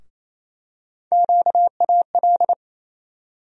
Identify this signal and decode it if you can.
Morse code, heard as a keyed tone, about 28 wpm, 700 Hz. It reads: QAL
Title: USA Call Sign Postfixes - 3 Letters 28wpm